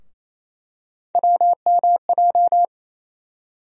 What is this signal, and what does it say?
Morse code, heard as a keyed tone, about 28 wpm, 700 Hz. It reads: WMJ
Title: USA Call Sign Postfixes - 3 Letters 28wpm